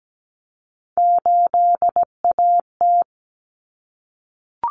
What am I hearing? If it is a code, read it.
Morse code, heard as a keyed tone, about 17 wpm, 690 Hz. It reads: 8AT
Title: Common Cut Numbers - 3-Digit 17wpm